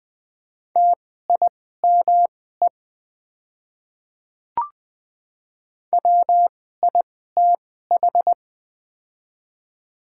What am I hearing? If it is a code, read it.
Morse code, heard as a keyed tone, about 20 wpm, spaced as for 14 wpm, 700 Hz. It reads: TIME WITH
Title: 2x Character Spacing - Top 200 Words - 20wpm